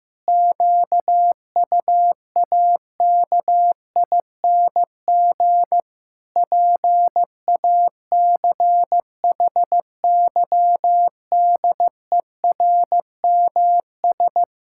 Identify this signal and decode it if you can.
Morse code, heard as a keyed tone, about 16 wpm, 700 Hz. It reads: QUAKING PACHYDERMS
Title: Pangram Challenge v2 - 15wpm